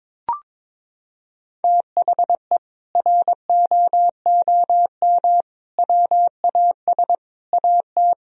Morse code, heard as a keyed tone, about 22 wpm, 700 Hz. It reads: THE ROOM WAS AT
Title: Sentences from Top 1000 Words - Review Part 2 - 22wpm